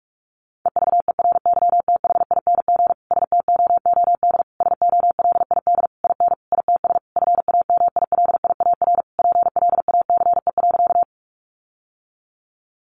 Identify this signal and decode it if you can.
Morse code, heard as a keyed tone, about 44 wpm, 700 Hz. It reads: EVERYTHING STOOD SOLID IN ITS FAMILIAR PLACE.
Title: Sentences from Top 1500 Words 45wpm